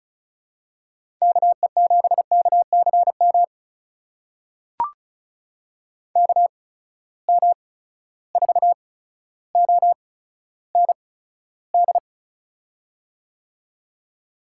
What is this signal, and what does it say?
Morse code, heard as a keyed tone, about 36 wpm, spaced as for 15 wpm, 700 Hz. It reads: KE7KCM KM4OND
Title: USA Call Signs - 6 Characters - 8x Character Spacing - 35wpm